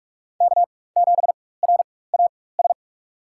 Morse code, heard as a keyed tone, about 45 wpm, spaced as for 21 wpm, 700 Hz. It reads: K7RAS
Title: USA Call Signs - 5 Characters - 4x Character Spacing - 45wpm